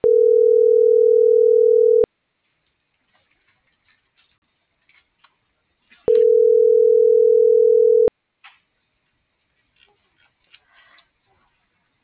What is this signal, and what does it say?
Ringback tone (North America), 440 + 480 Hz.